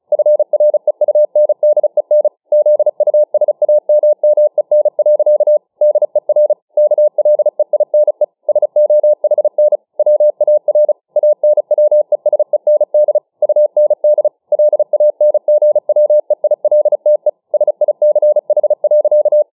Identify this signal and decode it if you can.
Morse code, heard as a keyed tone, about 34 wpm, 600 Hz. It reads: FREUNDEN ZUSAMMEN. DER KLEINE SOHN WAR ANWESEND UND LANGWEILTE SICH.